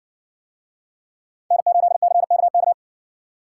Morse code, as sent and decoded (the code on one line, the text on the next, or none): -. ---.. -.-. -.- -.-
N8CKK